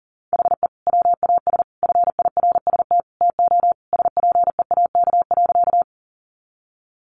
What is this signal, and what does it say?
Morse code, heard as a keyed tone, about 37 wpm, 700 Hz. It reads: HE WAS FIRST TO SPEAK.
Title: Sentences from Top 1100 Words 40wpm